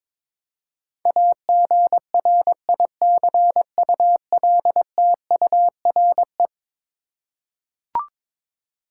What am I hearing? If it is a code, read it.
Morse code, heard as a keyed tone, about 21 wpm, 700 Hz. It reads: AGRICULTURE
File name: practice / 22wpm / Eleven Letter Words - 22wpm